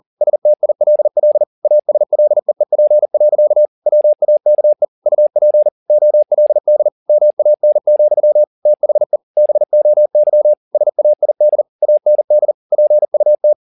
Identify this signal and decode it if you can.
Morse code, heard as a keyed tone, about 40 wpm, 600 Hz. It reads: STILL ASLEEP. WAKE UP OLD MAN, THE BOY SAID AND PUT